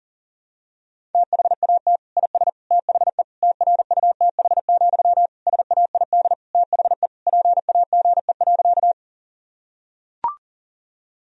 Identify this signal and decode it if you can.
Morse code, heard as a keyed tone, about 40 wpm, 700 Hz. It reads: THAT IS THE TRUTH, SAID THE PAGE.
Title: Sentences from Top 1000 Words - Review Part 1 - 40wpm